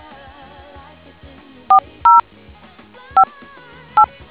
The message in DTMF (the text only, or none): 4058